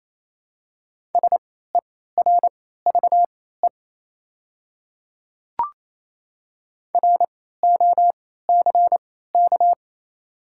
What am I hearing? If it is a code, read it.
Morse code, heard as a keyed tone, about 28 wpm, spaced as for 16 wpm, 700 Hz. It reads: SERVE ROCK
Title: 3x Character Spacing - Top 500 Words - 28wpm